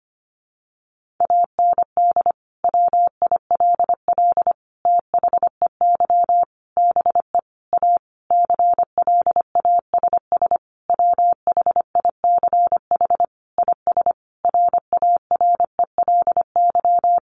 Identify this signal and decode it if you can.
Morse code, heard as a keyed tone, about 25 wpm, 700 Hz. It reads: AND WILL THEY BE A CLASS WHICH IS RARELY
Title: Sentences from Top 1400 Words 25wpm